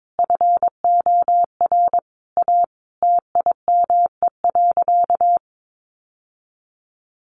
Morse code, heard as a keyed tone, about 22 wpm, 700 Hz. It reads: FOR A TIME.
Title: Sentences from Top 700 Words 22wpm